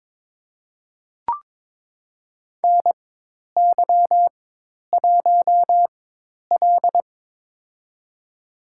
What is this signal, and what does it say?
Morse code, heard as a keyed tone, about 23 wpm, spaced as for 10 wpm, 700 Hz. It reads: NY1L
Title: USA Call Signs - 4 Characters - 4x Character Spacing - 22wpm